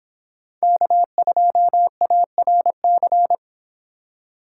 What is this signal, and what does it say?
Morse code, heard as a keyed tone, about 24 wpm, 700 Hz. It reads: K2ARC